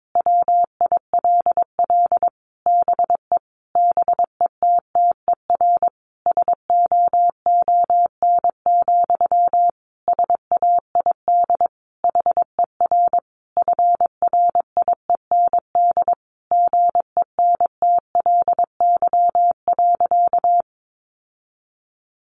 Morse code, heard as a keyed tone, about 22 wpm, 700 Hz. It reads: WILL BE BETTER SOON, SAID HER FRIEND GENTLY.
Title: Sentences from Top 1500 Words 22wpm